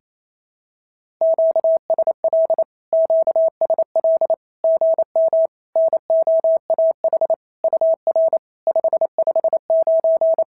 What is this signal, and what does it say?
Morse code, heard as a keyed tone, about 28 wpm, 650 Hz. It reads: QSL QSL GM NOAH UR 559
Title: SOTA QSO Practice 28wpm